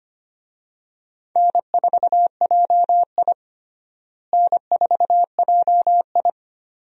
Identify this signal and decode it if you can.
Morse code, heard as a keyed tone, about 25 wpm, 700 Hz. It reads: N4JI N4JI
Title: USA Call Signs - 4 Characters - Repeated 2x - 25wpm